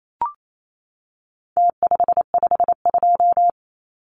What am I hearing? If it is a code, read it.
Morse code, heard as a keyed tone, about 28 wpm, 700 Hz. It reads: T552
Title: Numbers - Contest Style - 4-Digit 28wpm